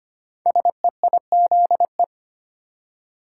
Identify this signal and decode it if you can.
Morse code, heard as a keyed tone, about 25 wpm, 700 Hz. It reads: SEIZE
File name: 5 Letter Words - First 3 Letters Given - 25wpm